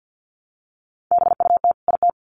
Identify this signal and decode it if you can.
Morse code, heard as a keyed tone, about 50 wpm, 700 Hz. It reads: BUT IT
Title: Sets of 2 Words 50wpm